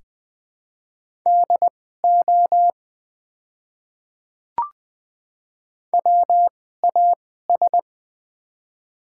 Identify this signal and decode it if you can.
Morse code, heard as a keyed tone, about 20 wpm, spaced as for 14 wpm, 700 Hz. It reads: DO WAS
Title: 2x Character Spacing - Top 100 Words - 20wpm